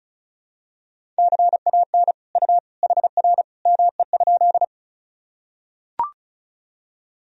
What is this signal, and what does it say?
Morse code, heard as a keyed tone, about 34 wpm, 700 Hz. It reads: CAN U HR ME?
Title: Sentences from Top 300 Words and Abbreviations 35wpm